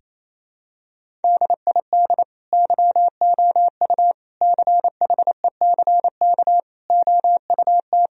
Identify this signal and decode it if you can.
Morse code, heard as a keyed tone, about 28 wpm, 700 Hz. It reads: DID YOU CHECK OUT
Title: Sentences from Top 1000 Words - Review Part 4 - 28wpm